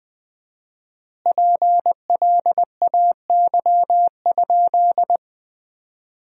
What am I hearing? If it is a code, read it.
Morse code, heard as a keyed tone, about 21 wpm, 700 Hz. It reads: PLAY?
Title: 4x Word-Spacing - Sentences from Top 200 Words - 20wpm